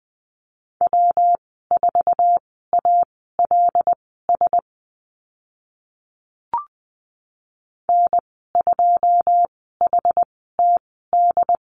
Morse code, heard as a keyed tone, about 20 wpm, spaced as for 14 wpm, 700 Hz. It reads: W4ALS N2HTD
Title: 147 Lesson 24g - Letter y - Call Signs 20wpm